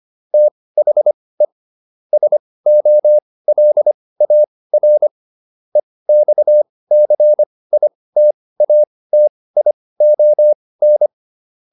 Morse code, heard as a keyed tone, about 25 wpm, spaced as for 18 wpm, 600 Hz. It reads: THE SOLAR EXCITATION